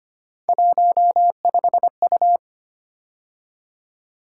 Morse code, heard as a keyed tone, about 29 wpm, 700 Hz. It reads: 15U
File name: Speed-Racing - Three Letter-Number Combinations - 25wpm